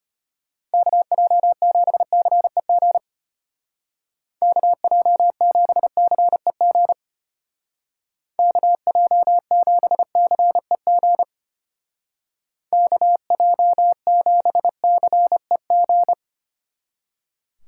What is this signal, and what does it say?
Morse code, heard as a keyed tone, about 33 wpm, 700 Hz. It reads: KJ7CEG KJ7CEG KJ7CEG KJ7CEG